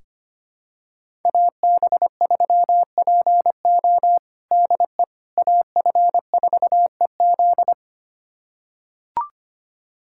Morse code, heard as a keyed tone, about 25 wpm, 700 Hz. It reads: AB3PO DE AF4EZ
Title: USA Callsign DE Callsign - 5 Characters - 25wpm